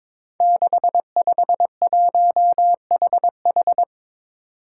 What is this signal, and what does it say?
Morse code, heard as a keyed tone, about 22 wpm, 700 Hz. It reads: 651HH
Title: Five Challenging Letter-Number Combinations 22wpm